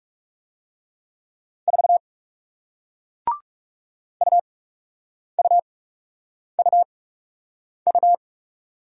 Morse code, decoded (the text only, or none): 4 UUUU